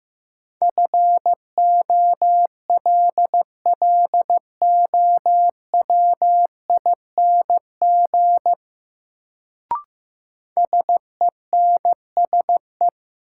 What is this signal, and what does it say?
Morse code, heard as a keyed tone, about 15 wpm, 700 Hz. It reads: FOLLOWING SENSE
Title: Sentences from Top 700 Words - New Word Drill - 15wpm